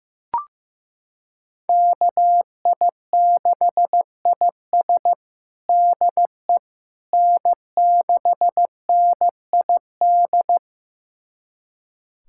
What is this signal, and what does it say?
Morse code, heard as a keyed tone, about 16 wpm, 700 Hz. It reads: KI6IS DE N6NID
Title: USA Callsign DE Callsign - 5 Characters - 15wpm